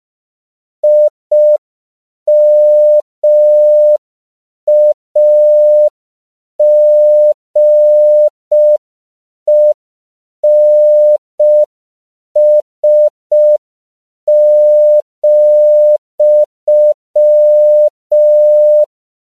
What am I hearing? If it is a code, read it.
Morse code, heard as a keyed tone, about 5 wpm, 600 Hz. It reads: IMAGENS,
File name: CW / 5PPM